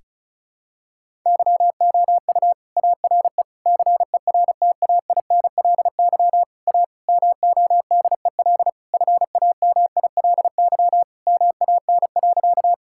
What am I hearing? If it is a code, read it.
Morse code, heard as a keyed tone, about 36 wpm, 700 Hz. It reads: YOU ARE CERTAINLY A MODEL FAMILY MAN.
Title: Sentences from Top 1000 Words - Review Part 5 - 35wpm